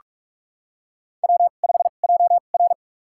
Morse code, heard as a keyed tone, about 46 wpm, spaced as for 33 wpm, 700 Hz. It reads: W5JR
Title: USA Call Signs - 4 Characters - 2x Character Spacing - 45wpm